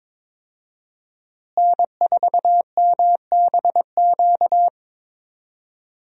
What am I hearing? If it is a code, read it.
Morse code, heard as a keyed tone, about 22 wpm, 700 Hz. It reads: N4MBQ
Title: USA Call Signs - 5 Characters - Repeated 3x - 22wpm